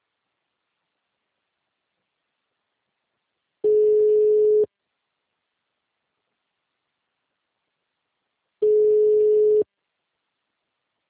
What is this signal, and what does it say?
Ringback tone (Europe, 425 Hz).
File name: out- -2002- - - .436